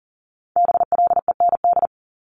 Morse code, heard as a keyed tone, about 41 wpm, 690 Hz. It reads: BLEND
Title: Multi-letter Phonemes 40wpm